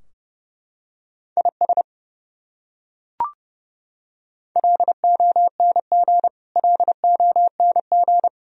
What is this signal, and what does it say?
Morse code, heard as a keyed tone, about 32 wpm, 700 Hz. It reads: IS LONG LONG
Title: Top 100 Words Repeated 30wpm